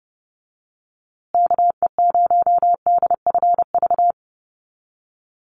Morse code, heard as a keyed tone, about 30 wpm, 700 Hz. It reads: KE0DFV
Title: USA Call Signs Repeated 3x 30wpm